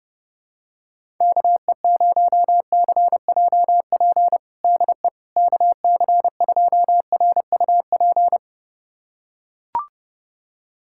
Morse code, heard as a keyed tone, about 30 wpm, 700 Hz. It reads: KE0CJP DE KC2RUP
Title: USA Callsign DE Callsign - 6 Characters - 30wpm